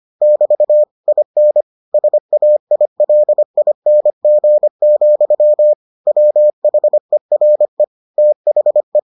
Morse code, heard as a keyed tone, about 25 wpm, 600 Hz. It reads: <BT> IN SAILING, WHERE THE